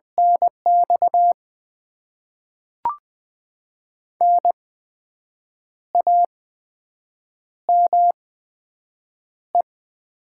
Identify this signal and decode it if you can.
Morse code, heard as a keyed tone, about 20 wpm, spaced as for 7 wpm, 700 Hz. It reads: NX NAME
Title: Top 20 QSO Elements - 8x Character Spacing - 20wpm